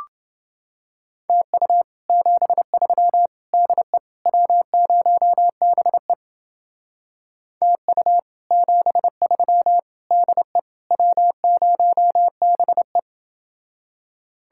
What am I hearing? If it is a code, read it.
Morse code, heard as a keyed tone, about 29 wpm, 700 Hz. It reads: TU 73 DE W0BE TU 73 DE W0BE